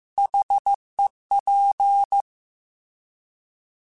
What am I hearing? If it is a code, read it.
Morse code, heard as a keyed tone, about 15 wpm, 800 Hz. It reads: HEP
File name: Alphabetical / Alphabetical mp3s / Hs @13